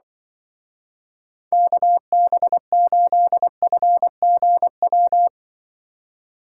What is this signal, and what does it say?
Morse code, heard as a keyed tone, about 24 wpm, 700 Hz. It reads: KB8FGW
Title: Speed Racing - USA 2x3 Call Signs 20wpm